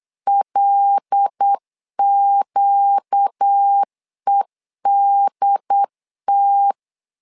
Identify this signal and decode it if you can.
Morse code, heard as a keyed tone, about 8 wpm, 800 Hz. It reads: LQEDT
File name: LEZIONI / 5 LBVC / 5 L 8wpm 2831